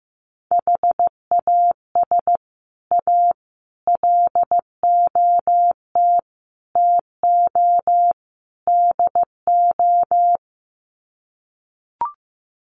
Morse code, heard as a keyed tone, about 15 wpm, 700 Hz. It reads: HAS A LOT TO DO
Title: Sets of 5 Words Encore2 15wpm